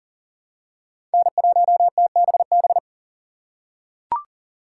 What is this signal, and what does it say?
Morse code, heard as a keyed tone, about 40 wpm, 700 Hz. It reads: N1TBB
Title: USA Call Signs - 5 Characters - 3x Character Spacing - 40wpm